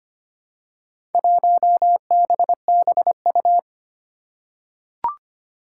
Morse code, heard as a keyed tone, about 25 wpm, 700 Hz. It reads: 1BBU
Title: Four Challenging Letter-Number Combinations 25wpm